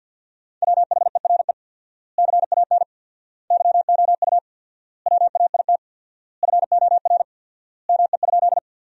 Morse code, heard as a keyed tone, about 51 wpm, 700 Hz. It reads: WHERE CAN YOU WAIT FOR ME?
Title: 4x Word-Spacing - Sentences from Top 500 Words - 50wpm